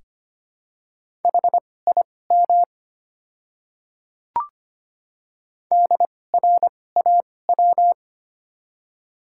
Morse code, heard as a keyed tone, about 25 wpm, spaced as for 18 wpm, 700 Hz. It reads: HIM DRAW
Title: 2x Character Spacing - Top 500 Words - 25wpm